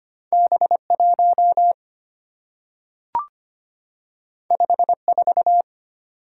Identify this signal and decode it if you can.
Morse code, heard as a keyed tone, about 25 wpm, 700 Hz. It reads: B1 54
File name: Two Letter-Number Combinations - 25wpm